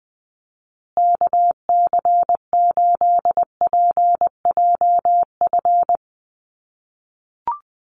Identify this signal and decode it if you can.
Morse code, heard as a keyed tone, about 20 wpm, 700 Hz. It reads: KC8PJF
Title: USA Call Signs Repeated 3x 20wpm